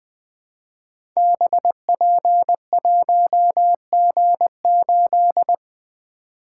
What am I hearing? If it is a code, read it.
Morse code, heard as a keyed tone, about 20 wpm, 700 Hz. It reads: BP1G8
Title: Code Groups - 5 Letter:Numbers - 20wpm